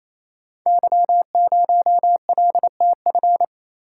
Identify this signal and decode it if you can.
Morse code, heard as a keyed tone, about 28 wpm, 700 Hz. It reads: Y0LTF
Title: Code Groups - 5 Letter:Numbers - 28wpm